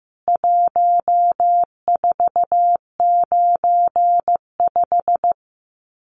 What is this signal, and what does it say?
Morse code, heard as a keyed tone, about 15 wpm, 700 Hz. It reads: 1495
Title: CWT Exchange 15wpm